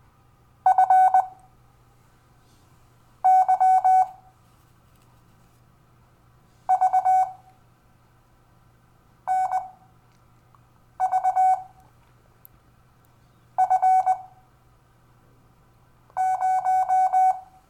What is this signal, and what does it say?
Morse code, heard as a keyed tone, about 20 wpm, spaced as for 4 wpm, 750 Hz. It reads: FYVNVF0
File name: cw / Lesson12 YV